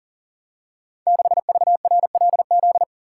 Morse code, heard as a keyed tone, about 42 wpm, 700 Hz. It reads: 6VRLZ
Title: Code Groups - 5 Letter:Numbers - 40wpm